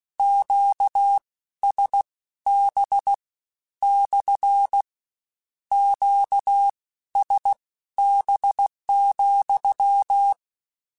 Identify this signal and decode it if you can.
Morse code, heard as a keyed tone, about 16 wpm, spaced as for 12 wpm, 800 Hz. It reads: QSB / QSB,
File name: Q Calls / Q Calls-mp3s / Q Calls SPSL2 @15